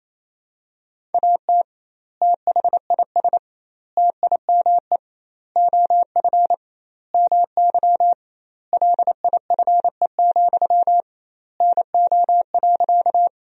Morse code, heard as a keyed tone, about 28 wpm, 700 Hz. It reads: AT THIS TIME OF MY LIFE, NO.
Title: 2x Word-Spacing - Sentences from Top 300 Words - 28wpm